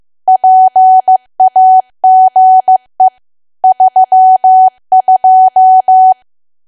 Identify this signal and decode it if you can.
Morse code, heard as a keyed tone, about 15 wpm, 750 Hz. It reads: PAGE 32
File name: cwa / W1AW-Level2 / 15WPM